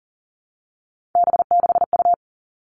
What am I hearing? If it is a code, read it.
Morse code, heard as a keyed tone, about 40 wpm, 700 Hz. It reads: B6U
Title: Three Challenging Letter-Number Combinations 40wpm